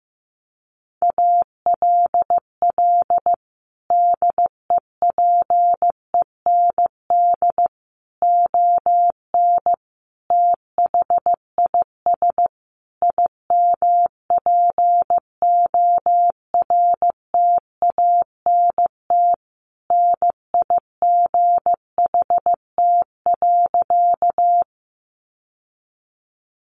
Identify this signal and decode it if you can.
Morse code, heard as a keyed tone, about 15 wpm, 700 Hz. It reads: ALL DEPEND ON THIS IMPORTANT NIGHT.
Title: Sentences from Top 1300 Words 15wpm